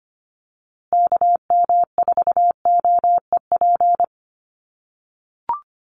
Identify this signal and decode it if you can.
Morse code, heard as a keyed tone, about 25 wpm, 700 Hz. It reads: KM4OEP
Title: USA Call Signs Repeated 3x 25wpm